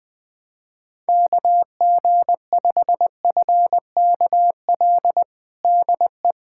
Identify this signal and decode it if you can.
Morse code, heard as a keyed tone, about 20 wpm, 700 Hz. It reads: KG5FKL DE
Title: USA Callsign DE Callsign 20wpm